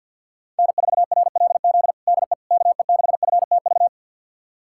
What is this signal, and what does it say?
Morse code, heard as a keyed tone, about 50 wpm, 700 Hz. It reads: N4RLZ DE KE6FTV
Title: USA Callsign DE Callsign 50wpm